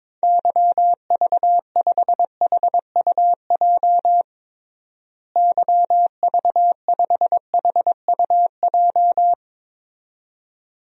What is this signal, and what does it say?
Morse code, heard as a keyed tone, about 23 wpm, 700 Hz. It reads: YV5HUJ YV5HUJ
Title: DXCC-100 Call Signs - 6 Characters - Repeated 2x - 22wpm